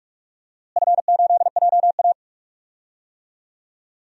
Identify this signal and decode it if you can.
Morse code, heard as a keyed tone, about 44 wpm, 700 Hz. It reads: F8JA
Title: DXCC-100 Call Signs - 4 Characters - Repeated 3x - 45wpm